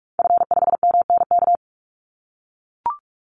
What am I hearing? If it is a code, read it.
Morse code, heard as a keyed tone, about 44 wpm, 700 Hz. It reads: F5MNK
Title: Call Signs - 45wpm